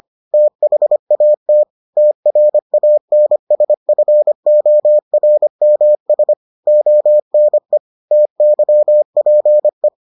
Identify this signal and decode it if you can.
Morse code, heard as a keyed tone, about 25 wpm, 600 Hz. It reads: THAT TRANSFORMS ONE TYPE